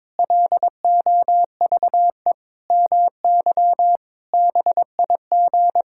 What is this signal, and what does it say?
Morse code, heard as a keyed tone, about 22 wpm, 700 Hz. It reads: LOVE MY BIG
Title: Pangram Challenge v2 - 22wpm